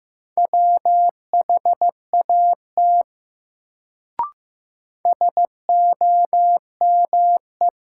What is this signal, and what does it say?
Morse code, heard as a keyed tone, about 15 wpm, 700 Hz. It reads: WHAT SOME
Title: Sentences from Top 100 Words - New Word Drill - 15wpm